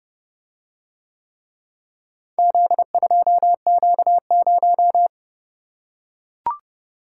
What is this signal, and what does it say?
Morse code, heard as a keyed tone, about 28 wpm, 700 Hz. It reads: Z2Q0